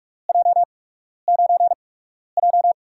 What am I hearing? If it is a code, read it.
Morse code, heard as a keyed tone, about 44 wpm, 700 Hz. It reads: J 9 J
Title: USA Call Signs - 4 Characters - 8x Character Spacing - 45wpm